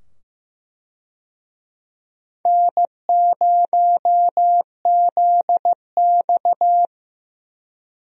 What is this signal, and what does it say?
Morse code, heard as a keyed tone, about 15 wpm, 700 Hz. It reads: N0ZX
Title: USA Call Signs - 4 Characters - 15wpm